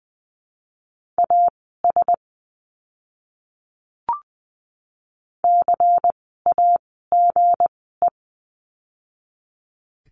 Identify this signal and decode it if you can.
Morse code, heard as a keyed tone, about 20 wpm, spaced as for 14 wpm, 700 Hz. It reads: AS CAGE
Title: 160 Lesson 26f - Letter g - Words 20wpm